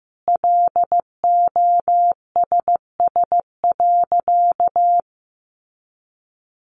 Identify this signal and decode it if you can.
Morse code, heard as a keyed tone, about 14 wpm, 700 Hz. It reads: LOSS.
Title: Sentences from Top 1400 Words 15wpm